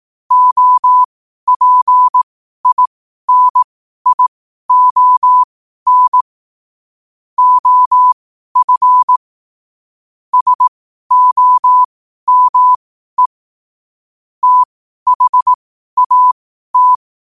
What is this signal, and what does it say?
Morse code, heard as a keyed tone, about 18 wpm, spaced as for 12 wpm, 1000 Hz. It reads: OPINION OF SOME THAT
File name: qotdfiles / 12wpm